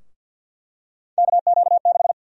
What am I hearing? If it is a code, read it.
Morse code, heard as a keyed tone, about 50 wpm, 700 Hz. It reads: KQ6
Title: USA Call Sign Prefixes - 50wpm